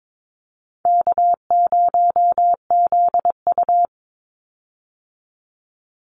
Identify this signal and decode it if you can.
Morse code, heard as a keyed tone, about 22 wpm, 700 Hz. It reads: K0ZU
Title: Easy Call Signs Repeated 2x - 22wpm